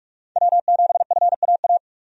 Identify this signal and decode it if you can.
Morse code, heard as a keyed tone, about 46 wpm, 700 Hz. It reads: W7FAA
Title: USA Call Signs - 5 Characters - 45wpm